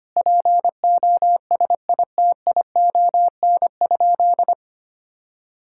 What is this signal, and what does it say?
Morse code, heard as a keyed tone, about 25 wpm, 700 Hz. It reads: POSITION?
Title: Sentences from Top 1000 Words - Review Part 5 - 25wpm